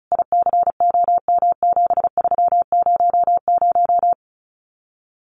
Morse code, heard as a keyed tone, about 34 wpm, 700 Hz. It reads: ICOM7300
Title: Common QSO Phrases Encore 35wpm